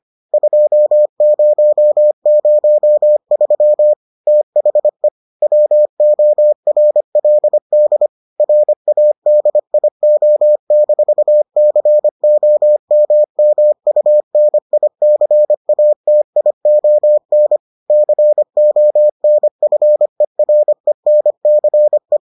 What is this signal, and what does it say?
Morse code, heard as a keyed tone, about 25 wpm, 600 Hz. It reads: 2003 THE WORLD RADIO-COMMUNICATION CONFERENCE